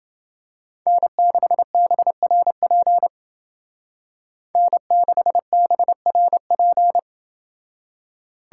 Morse code, heard as a keyed tone, about 28 wpm, 700 Hz. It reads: N6BRP N6BRP